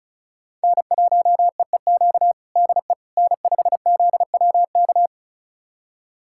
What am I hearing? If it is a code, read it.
Morse code, heard as a keyed tone, about 36 wpm, 700 Hz. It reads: N1EEQ DE N5ZWK
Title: USA Callsign DE Callsign - 5 Characters - 35wpm